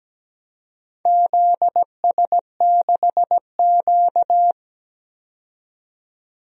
Morse code, heard as a keyed tone, about 17 wpm, 700 Hz. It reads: ZS6Q
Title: DXCC-100 Call Signs - 4 Characters - 17wpm